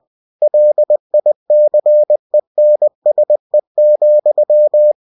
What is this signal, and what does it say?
Morse code, heard as a keyed tone, about 20 wpm, 600 Hz. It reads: LICENSE,
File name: cw-books / winnie-the-pooh / winnie-20wpm-page-0112